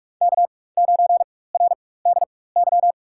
Morse code, heard as a keyed tone, about 47 wpm, spaced as for 21 wpm, 700 Hz. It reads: K9RDY
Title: USA Call Signs - 5 Characters - 4x Character Spacing - 45wpm